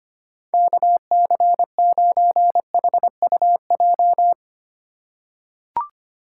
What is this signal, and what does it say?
Morse code, heard as a keyed tone, about 25 wpm, 700 Hz. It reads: KC9HUJ
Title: USA Call Signs - 6 Characters - 8x Character Spacing - 25wpm